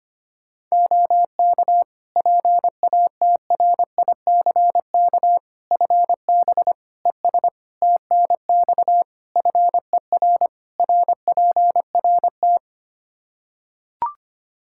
Morse code, heard as a keyed tone, about 25 wpm, 700 Hz. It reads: OK PATRICK FB ES TNX FER RPRT
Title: Practice QSO - Exchange 2 - Info - 25wpm